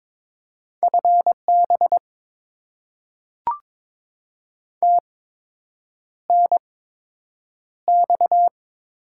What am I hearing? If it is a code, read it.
Morse code, heard as a keyed tone, about 22 wpm, spaced as for 8 wpm, 700 Hz. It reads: FB TNX
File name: Top 20 QSO Elements - 8x Character Spacing - 22wpm